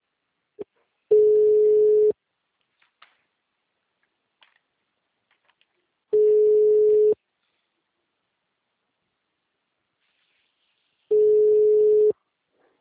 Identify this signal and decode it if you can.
Ringback tone (Europe, 425 Hz).